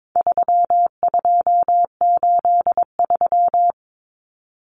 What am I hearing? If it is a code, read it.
Morse code, heard as a keyed tone, about 22 wpm, 700 Hz. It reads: 3283
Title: Common Cut Numbers - 4-Digit 22wpm